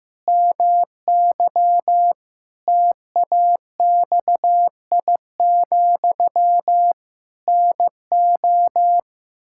Morse code, heard as a keyed tone, about 15 wpm, 700 Hz. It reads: MY TAXI, NO
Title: Pangram Challenge v2 - 15wpm